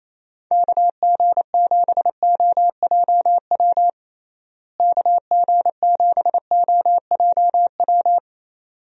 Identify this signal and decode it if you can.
Morse code, heard as a keyed tone, about 28 wpm, 700 Hz. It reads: KG7OJW KG7OJW
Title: USA Call Signs - 6 Characters - Repeated 2x - 28wpm